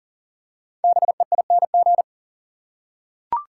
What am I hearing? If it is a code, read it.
Morse code, heard as a keyed tone, about 40 wpm, 700 Hz. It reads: BEING
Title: Five Letter Words - 40wpm